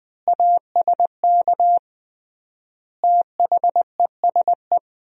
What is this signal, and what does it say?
Morse code, heard as a keyed tone, about 20 wpm, 700 Hz. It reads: ASK THESE
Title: 3x Word-Spacing - Sentences from Top 200 Words - 20wpm